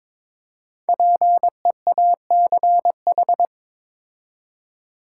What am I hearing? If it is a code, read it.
Morse code, heard as a keyed tone, about 21 wpm, 700 Hz. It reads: PEACH
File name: Difficult Words with Hints - 22wpm